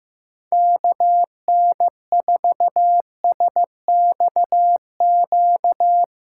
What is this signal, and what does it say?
Morse code, heard as a keyed tone, about 15 wpm, 700 Hz. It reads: KN4SXQ
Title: USA Callsign DE Callsign 15wpm